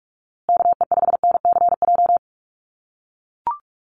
Morse code, heard as a keyed tone, about 44 wpm, 700 Hz. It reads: KE5NCJ
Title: USA Call Signs Repeated 3x 45wpm